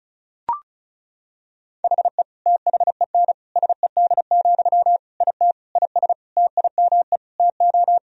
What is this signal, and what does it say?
Morse code, heard as a keyed tone, about 34 wpm, 700 Hz. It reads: HE THEN SED, IT IS TIME TO
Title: Sentences from Top 300 Words and Abbreviations 35wpm